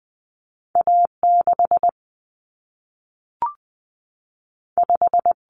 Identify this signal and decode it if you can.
Morse code, heard as a keyed tone, about 20 wpm, 700 Hz. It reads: A6 5
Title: Common Cut Numbers - 2-Digit 20wpm